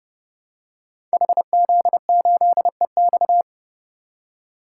Morse code, heard as a keyed tone, about 28 wpm, 700 Hz. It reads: HZ8EX
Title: Code Groups - 5 Letter:Numbers - 30wpm